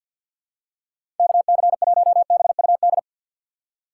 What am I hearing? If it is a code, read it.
Morse code, heard as a keyed tone, about 53 wpm, 700 Hz. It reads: KC1BUD
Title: USA Call Signs - 4x Character Spacing - 50wpm